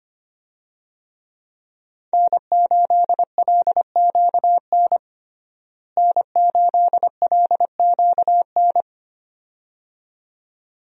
Morse code, heard as a keyed tone, about 25 wpm, 700 Hz. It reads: N8LQN N8LQN
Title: USA Call Signs - 5 Characters - Repeated 2x - 25wpm